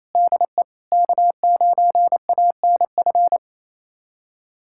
Morse code, heard as a keyed tone, about 28 wpm, 700 Hz. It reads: DE K9ANF
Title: USA Callsign DE Callsign - 5 Characters - 28wpm